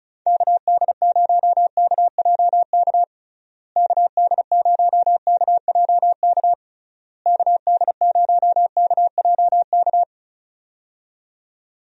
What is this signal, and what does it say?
Morse code, heard as a keyed tone, about 36 wpm, 700 Hz. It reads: KD0KJK KD0KJK KD0KJK
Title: USA Call Signs - 6 Characters - Repeated 3x - 35wpm